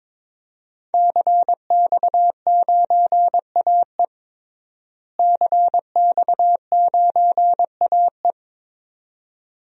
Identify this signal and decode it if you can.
Morse code, heard as a keyed tone, about 22 wpm, 700 Hz. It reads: CX9AE CX9AE
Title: DXCC-100 Call Signs - 5 Characters - Repeated 2x - 22wpm